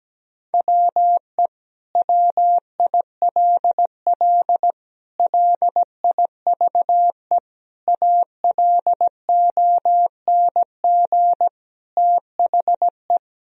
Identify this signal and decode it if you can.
Morse code, heard as a keyed tone, about 17 wpm, 700 Hz. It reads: WE WILL LIVE ALONG THE
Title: Sentences from Top 1000 Words - Review Part 4 - 17wpm